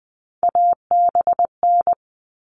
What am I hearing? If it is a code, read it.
Morse code, heard as a keyed tone, about 20 wpm, 700 Hz. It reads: ABN
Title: All Cut Numbers - 3-Digit 20wpm